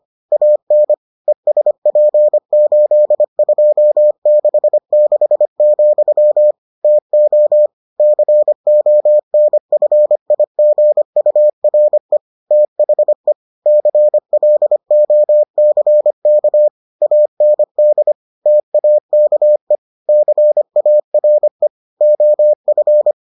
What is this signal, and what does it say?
Morse code, heard as a keyed tone, about 25 wpm, 600 Hz. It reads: AN ESP8266, TO CONFIGURE THE CLOCK AND TAKE CARE OF